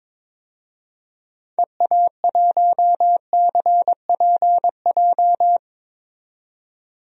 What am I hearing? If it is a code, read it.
Morse code, heard as a keyed tone, about 22 wpm, 700 Hz. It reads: EA1CPJ
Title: DXCC-100 Call Signs - 6 Characters - 22wpm